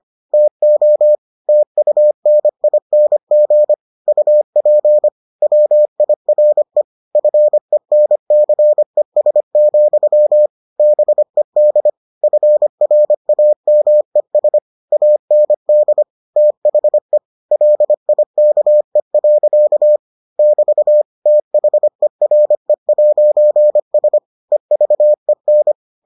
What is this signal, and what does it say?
Morse code, heard as a keyed tone, about 25 wpm, 600 Hz. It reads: TO TUNING UP WIRE FENCES, BED FRAMES AND THE LIKE. <BT> THERE'S EVEN